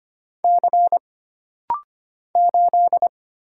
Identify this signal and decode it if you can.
Morse code, heard as a keyed tone, about 25 wpm, 700 Hz. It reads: C 8
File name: Single Character (Letter:Number:Punctuation:Prosign) - Rapid-Fire - 25wpm